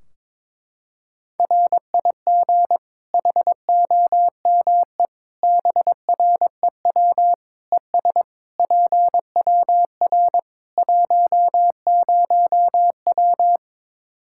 Morse code, heard as a keyed tone, about 22 wpm, 700 Hz. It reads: RIG HOME BREW ES PWR 10W
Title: Practice QSO - Exchange 2 - Info - 22wpm